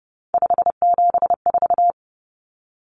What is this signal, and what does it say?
Morse code, heard as a keyed tone, about 30 wpm, 700 Hz. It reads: 574
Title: Numbers - 3-Digit 30wpm